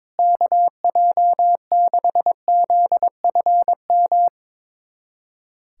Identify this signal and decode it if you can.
Morse code, heard as a keyed tone, about 21 wpm, 700 Hz. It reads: KJ6ZFM
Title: USA Callsign DE Callsign 22wpm